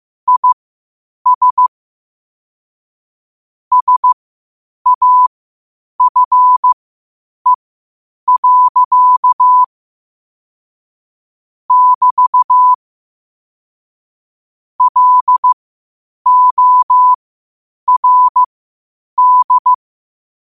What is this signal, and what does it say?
Morse code, heard as a keyed tone, about 15 wpm, spaced as for 8 wpm, 1000 Hz. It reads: IS SAFE. <BT> LORD
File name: qotdfiles / 8wpm